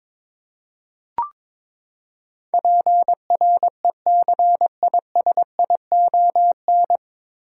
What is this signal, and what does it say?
Morse code, heard as a keyed tone, about 22 wpm, 700 Hz. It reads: PRECISION
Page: Morse Code Ninja practice 22wpm